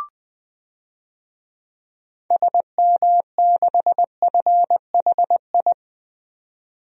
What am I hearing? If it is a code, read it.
Morse code, heard as a keyed tone, about 20 wpm, 700 Hz. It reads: SM6FHI
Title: DXCC-100 Call Signs - 6 Characters - 20wpm